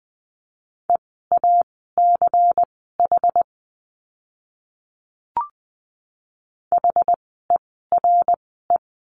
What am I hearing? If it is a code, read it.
Morse code, heard as a keyed tone, about 20 wpm, spaced as for 14 wpm, 700 Hz. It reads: EACH HERE
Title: 192 Lesson 31f - Letter v - Words 20wpm